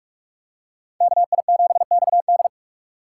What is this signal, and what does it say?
Morse code, heard as a keyed tone, about 44 wpm, 700 Hz. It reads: KI7XD